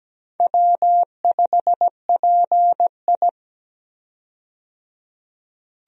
Morse code, heard as a keyed tone, about 17 wpm, 700 Hz. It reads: W5PI
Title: USA Call Signs - 4 Characters - 17wpm